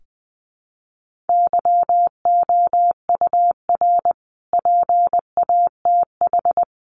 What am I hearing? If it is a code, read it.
Morse code, heard as a keyed tone, about 20 wpm, 700 Hz. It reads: YOUR PATH
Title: Sentences from Top 1400 Words 20wpm